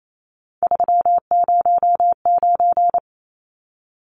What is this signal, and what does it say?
Morse code, heard as a keyed tone, about 28 wpm, 700 Hz. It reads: 309
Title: Numbers - 3-Digit 28wpm